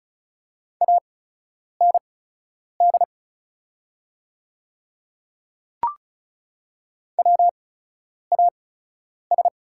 Morse code, heard as a keyed tone, about 33 wpm, spaced as for 9 wpm, 690 Hz. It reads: AND WAS
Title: 8x Character Spacing - Top 200 Words - 35wpm